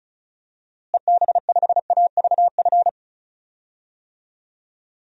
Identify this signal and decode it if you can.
Morse code, heard as a keyed tone, about 34 wpm, 700 Hz. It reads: EB5AVF
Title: DXCC-100 Call Signs - 6 Characters - Repeated 2x - 35wpm